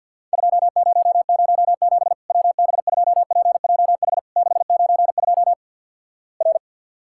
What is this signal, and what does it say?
Morse code, heard as a keyed tone, about 50 wpm, 700 Hz. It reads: 2007 WB2PJH 692 R
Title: Worked All Europe DX Contest - QTC Exchange with Repeats - 50wpm